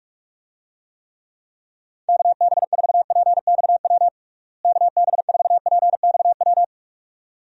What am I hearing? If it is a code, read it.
Morse code, heard as a keyed tone, about 46 wpm, 700 Hz. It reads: KB4PXW KB4PXW
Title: USA Call Signs - 6 Characters - Repeated 2x - 45wpm